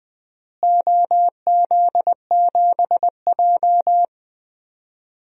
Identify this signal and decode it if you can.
Morse code, heard as a keyed tone, about 20 wpm, 700 Hz. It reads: OZ7J
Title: DXCC-100 Call Signs - 4 Characters - Repeated 2x - 20wpm